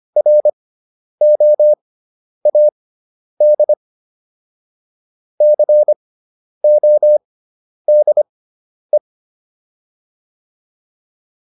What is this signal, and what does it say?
Morse code, heard as a keyed tone, about 25 wpm, spaced as for 10 wpm, 600 Hz. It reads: ROAD CODE